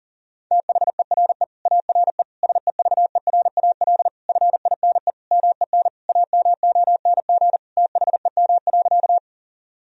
Morse code, heard as a keyed tone, about 40 wpm, 700 Hz. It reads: THERE ARE SEVERAL FINE MEN AMONG THEM.
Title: Sentences from Top 1000 Words - Review Part 2 - 40wpm